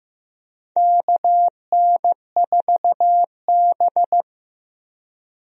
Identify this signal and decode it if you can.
Morse code, heard as a keyed tone, about 15 wpm, 700 Hz. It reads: KN4B
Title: USA Call Signs - 4 Characters - Repeated 3x - 15wpm